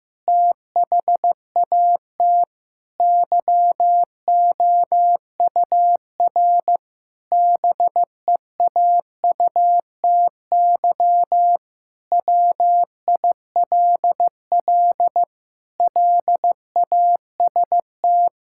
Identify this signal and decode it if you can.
Morse code, heard as a keyed tone, about 15 wpm, 700 Hz. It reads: THAT YOUR BEAUTY WILL LAST
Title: Sentences from Top 1600 Words 15wpm